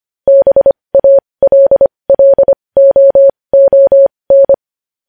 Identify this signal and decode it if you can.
Morse code, heard as a keyed tone, about 25 wpm, spaced as for 20 wpm, 550 Hz. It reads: BALLOON
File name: podcast / News headlines-1x-25x20